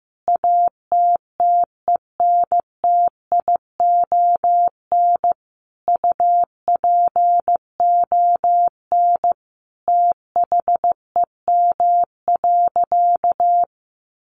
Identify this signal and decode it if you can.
Morse code, heard as a keyed tone, about 15 wpm, 700 Hz. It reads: ATTENTION UPON THEM.
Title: Sentences from Top 1400 Words 15wpm